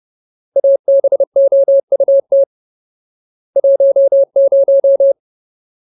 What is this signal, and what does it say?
Morse code, heard as a keyed tone, about 30 wpm, 550 Hz. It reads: ABOUT 10
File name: Ultimate Word List / 30WPM Ultimate